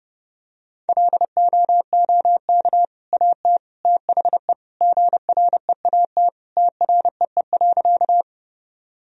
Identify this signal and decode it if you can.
Morse code, heard as a keyed tone, about 30 wpm, 700 Hz. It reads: LOOK AT THE GREAT TREE.
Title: Sentences from Top 1000 Words - Review Part 1 - 30wpm